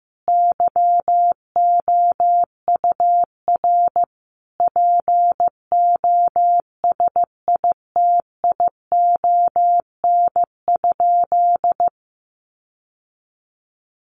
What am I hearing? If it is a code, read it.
Morse code, heard as a keyed tone, about 15 wpm, 700 Hz. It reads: YOUR POSITION?
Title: Sentences from Top 1400 Words 15wpm